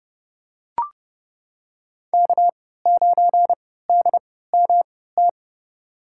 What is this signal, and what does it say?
Morse code, heard as a keyed tone, about 32 wpm, spaced as for 17 wpm, 700 Hz. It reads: K9DMT
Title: USA Call Signs - 5 Characters - 3x Character Spacing - 30wpm